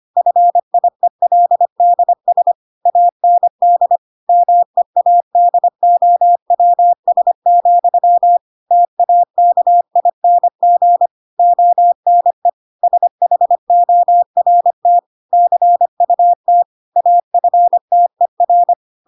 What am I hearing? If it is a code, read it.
Morse code, heard as a keyed tone, about 25 wpm, 700 Hz. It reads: FIELDS AND MEADOWS, TAKING ONE SHORT CUT AFTER
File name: Pinocchio / Pinocchio ch4 25 25 ik1hge